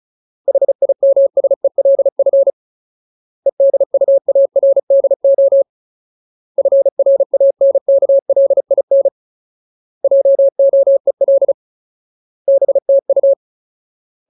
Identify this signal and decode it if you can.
Morse code, heard as a keyed tone, about 34 wpm, 550 Hz. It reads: HIMSELF EDUARDO FRANKLIN JOEL BTU
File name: Ultimate Word List / 35WPM Ultimate